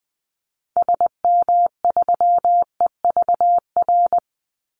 Morse code, heard as a keyed tone, about 20 wpm, 700 Hz. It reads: SM3EVR
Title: Call Signs - 20wpm